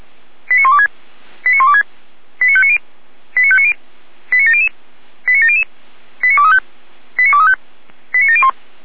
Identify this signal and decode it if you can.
Five-tone selective call (ZVEI), heard as quick runs of five tones, tones of about 70 ms. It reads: 89126 89126 89590 89590 89700 89700 89235 89235 89701
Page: /455/SDR/ZVEI/